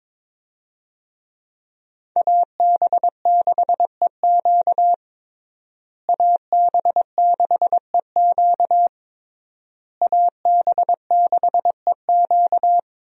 Morse code, heard as a keyed tone, about 22 wpm, 700 Hz. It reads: AB6EQ AB6EQ AB6EQ
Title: USA Call Signs - 5 Characters - Repeated 3x - 22wpm